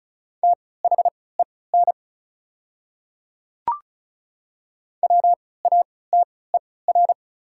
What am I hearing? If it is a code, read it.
Morse code, heard as a keyed tone, about 34 wpm, spaced as for 20 wpm, 690 Hz. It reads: THEN WATER
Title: 3x Character Spacing - Top 100 Words - 35wpm